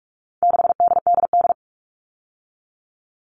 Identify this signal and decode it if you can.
Morse code, heard as a keyed tone, about 44 wpm, 700 Hz. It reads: 6DDD
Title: All Cut Numbers - 4-Digit 45wpm